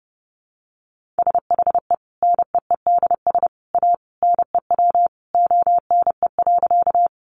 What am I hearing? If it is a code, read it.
Morse code, heard as a keyed tone, about 30 wpm, 700 Hz. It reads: SHE NEEDS A NEW ONE.
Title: Additional Easy Phrases 30wpm